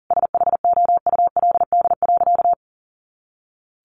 Morse code, heard as a keyed tone, about 40 wpm, 700 Hz. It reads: SHOULD.
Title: Sentences from Top 700 Words 40wpm